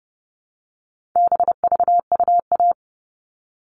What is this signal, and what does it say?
Morse code, heard as a keyed tone, about 30 wpm, 700 Hz. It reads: BVUA
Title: All Cut Numbers - 4-Digit 30wpm